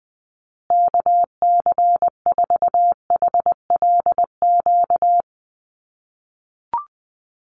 Morse code, heard as a keyed tone, about 20 wpm, 700 Hz. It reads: KC4HLQ
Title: USA Call Signs Repeated 3x 20wpm